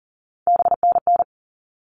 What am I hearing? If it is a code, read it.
Morse code, heard as a keyed tone, about 40 wpm, 700 Hz. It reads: BNN
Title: All Cut Numbers - 3-Digit 40wpm